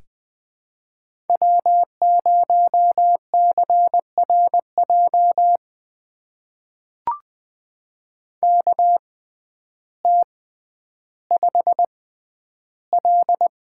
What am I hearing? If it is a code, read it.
Morse code, heard as a keyed tone, about 20 wpm, spaced as for 11 wpm, 700 Hz. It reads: W0CRJ KT5L
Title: USA Call Signs - 6x Character Spacing - 20wpm